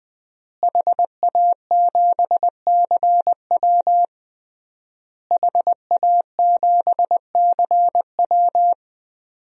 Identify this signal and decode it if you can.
Morse code, heard as a keyed tone, about 21 wpm, 700 Hz. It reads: HA7CW HA7CW
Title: DXCC-100 Call Signs - 5 Characters - Repeated 3x - 20wpm